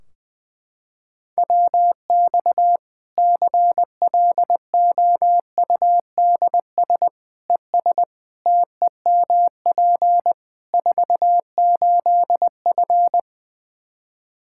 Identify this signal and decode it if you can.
Morse code, heard as a keyed tone, about 20 wpm, 700 Hz. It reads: WX CLOUDS ES TEMP 48F
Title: Practice QSO - Exchange 2 - Info - 20wpm